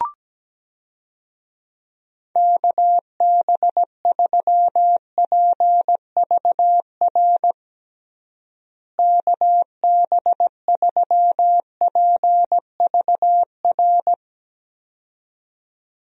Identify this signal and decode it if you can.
Morse code, heard as a keyed tone, about 17 wpm, 700 Hz. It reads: KB3PVR KB3PVR
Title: USA Call Signs - 6 Characters - Repeated 2x - 17wpm